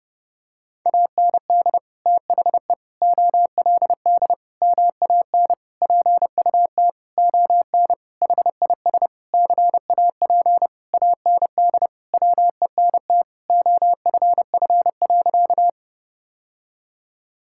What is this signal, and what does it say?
Morse code, heard as a keyed tone, about 30 wpm, 700 Hz. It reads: AND THE OLD MAN PUT ON HIS CAP AND WENT OFF.
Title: Sentences from Top 1600 Words 30wpm